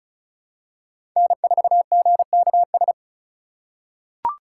Morse code, heard as a keyed tone, about 36 wpm, 700 Hz. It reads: N4GKS
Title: USA Call Signs - 5 Characters - 3x Character Spacing - 35wpm